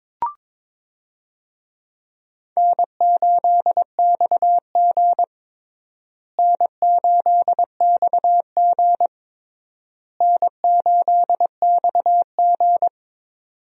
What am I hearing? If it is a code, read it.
Morse code, heard as a keyed tone, about 21 wpm, 700 Hz. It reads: N8XG N8XG N8XG
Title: USA Call Signs - 4 Characters - Repeated 3x - 22wpm